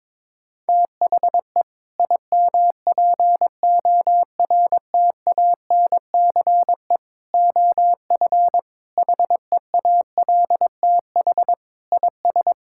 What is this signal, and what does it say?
Morse code, heard as a keyed tone, about 22 wpm, 700 Hz. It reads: THE IMPORTANCE OF HEALTH IS
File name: practice / 22wpm / Sentences with 3-Word Phrases - 22wpm